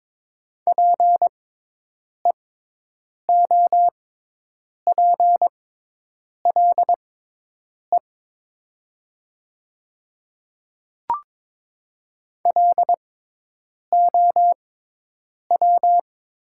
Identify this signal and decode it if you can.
Morse code, heard as a keyed tone, about 22 wpm, spaced as for 8 wpm, 700 Hz. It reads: PEOPLE LOW
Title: 6x Character Spacing - Top 200 Words - 22wpm